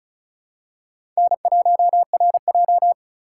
Morse code, heard as a keyed tone, about 34 wpm, 700 Hz. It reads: N1RJ
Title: USA Call Signs - 4 Characters - 2x Character Spacing - 35wpm